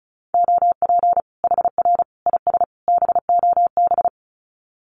Morse code, heard as a keyed tone, about 34 wpm, 700 Hz. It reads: OP HR IS BOB
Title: Common QSO Phrases 35wpm